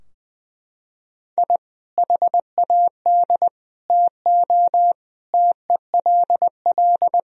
Morse code, heard as a keyed tone, about 20 wpm, 700 Hz. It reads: I HAD TO TELL
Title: Sentences from Top 1000 Words - Review Part 4 - 20wpm